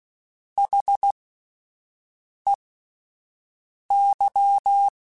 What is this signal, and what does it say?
Morse code, heard as a keyed tone, about 16 wpm, spaced as for 5 wpm, 800 Hz. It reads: HEY
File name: Alphabetical / Alphabetical mp3s / Hs @5